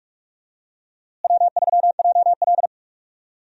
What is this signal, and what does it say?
Morse code, heard as a keyed tone, about 46 wpm, 700 Hz. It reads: W3JL
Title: USA Call Signs - 4 Characters - Repeated 3x - 45wpm